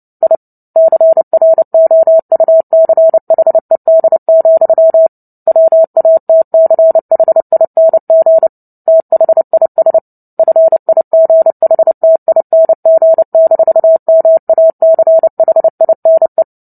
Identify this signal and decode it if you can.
Morse code, heard as a keyed tone, about 30 wpm, 650 Hz. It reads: I CROUCHED, WATCHING THIS FIGHTING-MACHINE
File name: Book Two / 03-Chapter 3 - 29 WPM